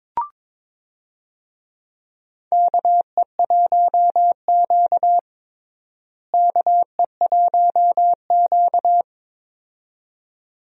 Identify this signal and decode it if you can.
Morse code, heard as a keyed tone, about 21 wpm, 700 Hz. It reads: KE1Q KE1Q
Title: USA Call Signs - 4 Characters - Repeated 2x - 22wpm